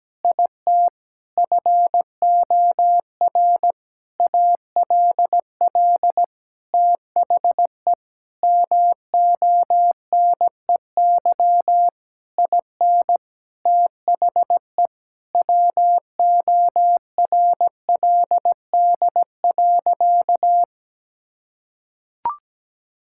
Morse code, heard as a keyed tone, about 17 wpm, 700 Hz. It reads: IT FOR ALL THE MONEY IN THE WORLD.
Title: Sentences from Top 1000 Words - Review Part 6 - 17wpm